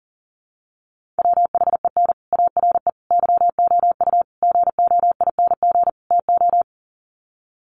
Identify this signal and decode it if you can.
Morse code, heard as a keyed tone, about 40 wpm, 700 Hz. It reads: WHEN ARE YOU GOING TO
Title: Sets of 5 Words Encore2 40wpm